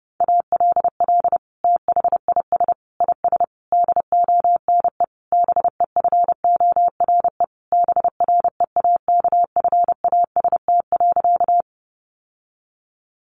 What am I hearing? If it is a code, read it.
Morse code, heard as a keyed tone, about 30 wpm, 700 Hz. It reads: ALL THIS IS DONE BEFORE BREAKFAST.
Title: Sentences from Top 1500 Words 30wpm